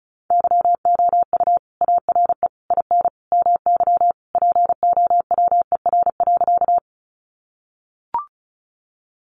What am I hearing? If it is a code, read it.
Morse code, heard as a keyed tone, about 34 wpm, 700 Hz. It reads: YOU ARE IN MY POWER.
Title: Sentences from Top 500 Words 35wpm